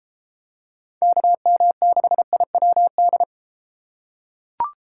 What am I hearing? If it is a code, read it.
Morse code, heard as a keyed tone, about 30 wpm, 700 Hz. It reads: KM6IWD